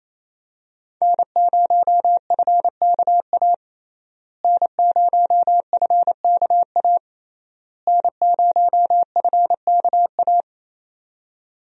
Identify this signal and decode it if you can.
Morse code, heard as a keyed tone, about 28 wpm, 700 Hz. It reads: N0FKA N0FKA N0FKA
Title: USA Call Signs - 5 Characters - Repeated 3x - 28wpm